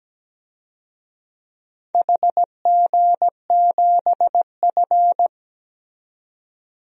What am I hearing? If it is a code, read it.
Morse code, heard as a keyed tone, about 17 wpm, 700 Hz. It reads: HG7F
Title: DXCC-100 Call Signs - 4 Characters - 17wpm